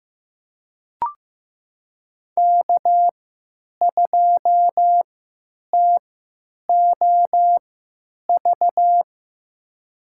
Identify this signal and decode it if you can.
Morse code, heard as a keyed tone, about 15 wpm, spaced as for 9 wpm, 700 Hz. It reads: K2TOV
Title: USA Call Signs - 5 Characters - 3x Character Spacing - 15wpm